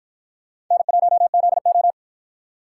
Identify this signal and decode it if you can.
Morse code, heard as a keyed tone, about 44 wpm, 700 Hz. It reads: N1ZO